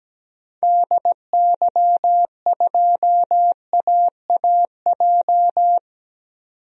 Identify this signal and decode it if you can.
Morse code, heard as a keyed tone, about 17 wpm, 700 Hz. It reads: DY2AAJ
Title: DXCC-100 Call Signs - 6 Characters - 17wpm